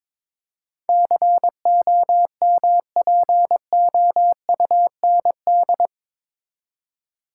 Morse code, heard as a keyed tone, about 22 wpm, 700 Hz. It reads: COMPOUND
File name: practice / 22wpm / Eight Letter Words - 22wpm